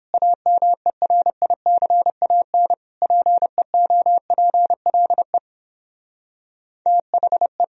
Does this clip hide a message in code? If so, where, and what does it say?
Yes: Morse code from 0.1 to 7.6 s, heard as a keyed tone, about 29 wpm, 700 Hz. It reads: AMERICAN PEOPLE THE